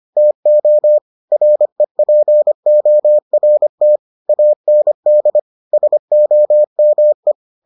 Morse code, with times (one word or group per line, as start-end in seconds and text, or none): 0.2-1.0 TO
1.3-4.0 REPORT
4.3-5.4 AND
5.7-7.3 SOME